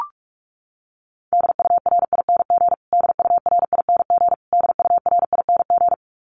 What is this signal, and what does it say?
Morse code, heard as a keyed tone, about 44 wpm, 700 Hz. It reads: DURING DURING DURING
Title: Top 500 Words Repeated 45wpm